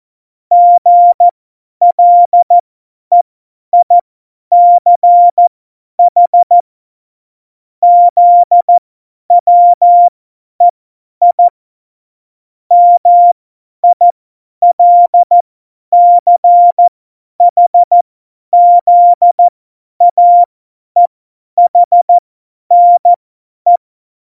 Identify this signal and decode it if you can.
Morse code, heard as a keyed tone, about 14 wpm, spaced as for 10 wpm, 700 Hz. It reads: GLEICH ZWEI MILCHZAEHNE
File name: Morsekurs / Witze / Witz 24. Oktober 17, Tempo 50